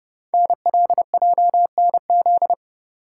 Morse code, heard as a keyed tone, about 30 wpm, 700 Hz. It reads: NLJNZ
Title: Code Groups - 5 Letters - 30wpm